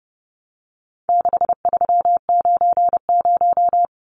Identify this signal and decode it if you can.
Morse code, heard as a keyed tone, about 30 wpm, 700 Hz. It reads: 6390
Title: Numbers - 4-Digit 30wpm